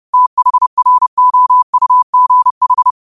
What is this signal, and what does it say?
Morse code, heard as a keyed tone, about 30 wpm, 1000 Hz. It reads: THROUGH